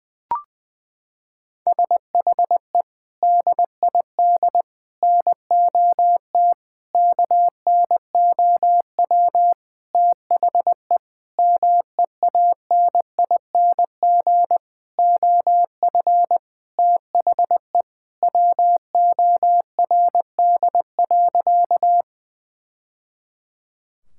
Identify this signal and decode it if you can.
Morse code, heard as a keyed tone, about 20 wpm, 700 Hz. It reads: SHE DID NOT KNOW THE MEANING OF THE WORD.
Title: Sentences from Top 1000 Words - Review Part 6 - 20wpm